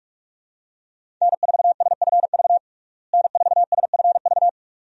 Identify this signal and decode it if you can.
Morse code, heard as a keyed tone, about 43 wpm, 700 Hz. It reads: N4SFV N4SFV
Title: USA Call Signs - 5 Characters - Repeated 2x - 45wpm